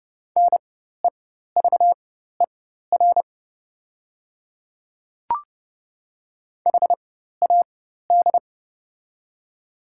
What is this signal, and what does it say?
Morse code, heard as a keyed tone, about 30 wpm, spaced as for 14 wpm, 700 Hz. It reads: NEVER HAD
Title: 4x Character Spacing - Top 300 Words - 30wpm